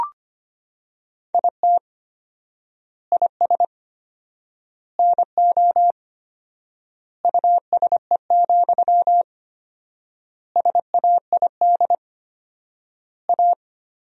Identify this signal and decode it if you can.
Morse code, heard as a keyed tone, about 25 wpm, 700 Hz. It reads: IT IS NO USE, SAID A
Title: 4x Word-Spacing - Sentences from Top 500 Words - 25wpm